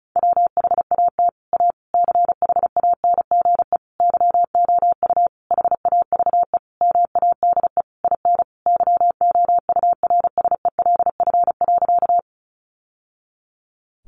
Morse code, heard as a keyed tone, about 34 wpm, 700 Hz. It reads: WHAT A CHANGE YOU HAVE MADE IN YOURSELF.
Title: Sentences from Top 600 Words 35wpm